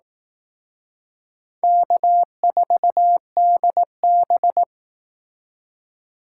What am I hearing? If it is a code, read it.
Morse code, heard as a keyed tone, about 18 wpm, 700 Hz. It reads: K4DB
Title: Speed Racing - USA 1x2 Call Signs 15wpm